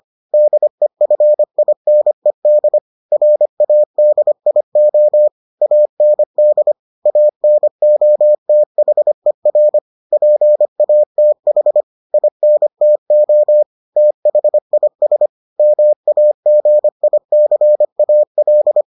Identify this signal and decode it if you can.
Morse code, heard as a keyed tone, about 25 wpm, 600 Hz. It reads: DEFINED RADIO AND ANOTHER PATH INTO THIS MAGICAL